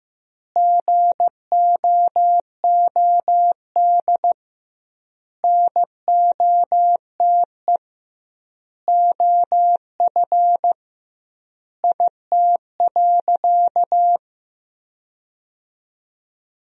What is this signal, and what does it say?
Morse code, heard as a keyed tone, about 15 wpm, 700 Hz. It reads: GOOD NOTE OF IT.
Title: 2x Word-Spacing - Sentences from Top 500 Words - 15wpm